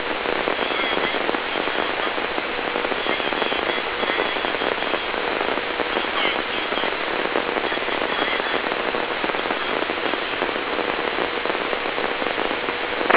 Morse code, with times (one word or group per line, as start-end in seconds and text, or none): none